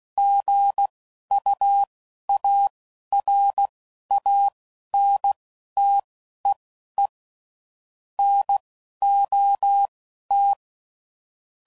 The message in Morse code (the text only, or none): GUARANTEE NOT